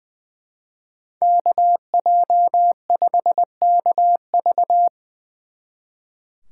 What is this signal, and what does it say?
Morse code, heard as a keyed tone, about 25 wpm, 700 Hz. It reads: KJ5KV
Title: Speed Racing - USA 2x2 Call Signs 20wpm